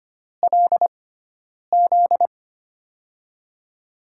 Morse code, heard as a keyed tone, about 25 wpm, spaced as for 9 wpm, 700 Hz. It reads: LZ
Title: USA Call Signs - 6x Character Spacing - 25wpm